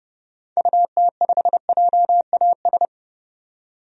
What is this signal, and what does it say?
Morse code, heard as a keyed tone, about 32 wpm, 700 Hz. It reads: UT5JAS
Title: DXCC-100 Call Signs - 6 Characters - 30wpm